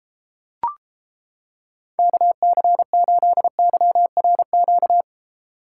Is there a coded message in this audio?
Yes: Morse code from 2.0 to 5.0 s, heard as a keyed tone, about 31 wpm, 700 Hz. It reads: KC8YRQ